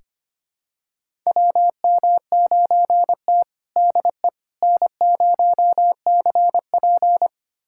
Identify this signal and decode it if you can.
Morse code, heard as a keyed tone, about 25 wpm, 700 Hz. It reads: WM9T DE N0CP
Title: USA Callsign DE Callsign - 4 Characters - 25wpm